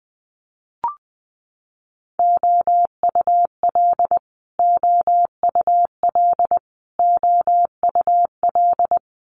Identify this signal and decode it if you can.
Morse code, heard as a keyed tone, about 20 wpm, 700 Hz. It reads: OUL OUL OUL
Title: Multi-letter Phonemes Repeated 3x 20wpm